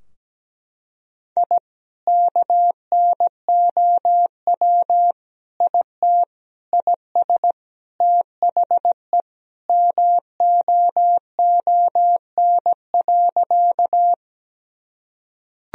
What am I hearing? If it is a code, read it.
Morse code, heard as a keyed tone, about 17 wpm, 700 Hz. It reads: I KNOW IT IS THE MOON.
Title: Sentences from Top 1000 Words - Review Part 3 - 17wpm